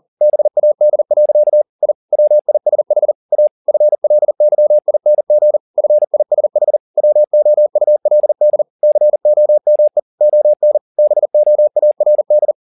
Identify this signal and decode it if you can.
Morse code, heard as a keyed tone, about 40 wpm, 600 Hz. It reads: BAD. I WISH A FLYING FISH WOULD COME ON BOARD